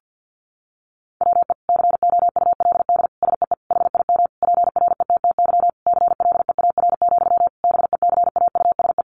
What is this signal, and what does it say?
Morse code, heard as a keyed tone, about 50 wpm, 700 Hz. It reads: WE COULD SEE HIM PRETTY CLEAR, BECAUSE
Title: Sentences from Top 1000 Words 50wpm